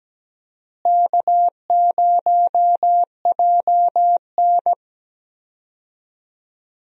Morse code, heard as a keyed tone, about 17 wpm, 700 Hz. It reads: K0JN
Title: USA Call Signs - 4 Characters - Repeated 3x - 17wpm